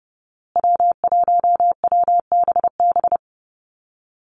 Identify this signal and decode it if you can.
Morse code, heard as a keyed tone, about 30 wpm, 700 Hz. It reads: W1WBB
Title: Call Signs Repeated 2x - 30wpm